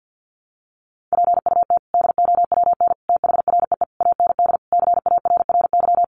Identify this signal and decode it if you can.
Morse code, heard as a keyed tone, about 48 wpm, 700 Hz. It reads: PUT DOWN THREE AND CARRY
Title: Sentences from Top 400 Words 50wpm